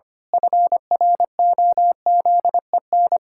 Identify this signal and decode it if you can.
Morse code, heard as a keyed tone, about 25 wpm, 700 Hz. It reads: FROZEN
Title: Pangram Challenge v2 - 25wpm